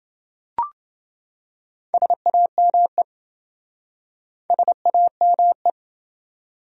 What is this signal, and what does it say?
Morse code, heard as a keyed tone, about 26 wpm, 700 Hz. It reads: SAME SAME